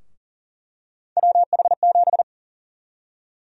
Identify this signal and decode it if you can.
Morse code, heard as a keyed tone, about 40 wpm, 700 Hz. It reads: WH7
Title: USA Call Sign Prefixes - Repeated 2x - 40wpm